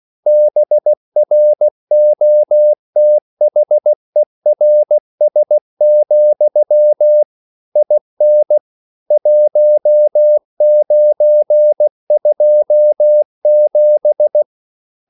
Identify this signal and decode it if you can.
Morse code, heard as a keyed tone, about 16 wpm, 600 Hz. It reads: BROTHERS, IN 1927
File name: CW / Practice / EnglishQuotations0416